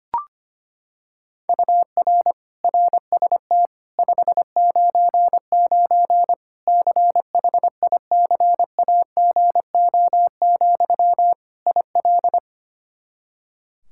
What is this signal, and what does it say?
Morse code, heard as a keyed tone, about 25 wpm, 700 Hz. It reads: UR RST 599 CHICAGO, IL
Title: Practice QSO - Exchange 1 - Beginning - 25wpm